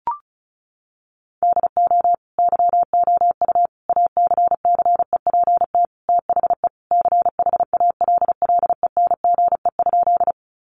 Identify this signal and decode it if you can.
Morse code, heard as a keyed tone, about 34 wpm, 700 Hz. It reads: DO YOU ACCEPT THE CHALLENGE?
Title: Sentences from Top 1300 Words 35wpm